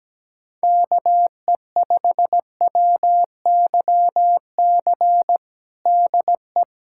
Morse code, heard as a keyed tone, about 17 wpm, 700 Hz. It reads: KE5WYC DE
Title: USA Callsign DE Callsign - 6 Characters - 17wpm